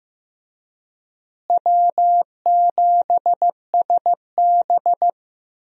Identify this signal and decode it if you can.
Morse code, heard as a keyed tone, about 15 wpm, 700 Hz. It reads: W7SB
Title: USA Call Signs - 4 Characters - Repeated 2x - 15wpm